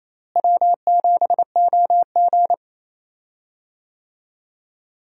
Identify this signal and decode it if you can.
Morse code, heard as a keyed tone, about 28 wpm, 700 Hz. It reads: W7OG
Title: USA Call Signs - 4 Characters - Repeated 3x - 28wpm